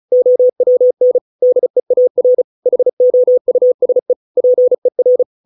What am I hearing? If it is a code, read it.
Morse code, heard as a keyed tone, about 34 wpm, 500 Hz. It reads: OWN DEAR HOUSE PER